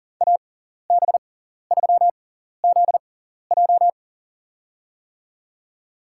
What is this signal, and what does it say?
Morse code, heard as a keyed tone, about 40 wpm, spaced as for 14 wpm, 700 Hz. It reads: AB3ZJ
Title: USA Call Signs - 5 Characters - 6x Character Spacing - 40wpm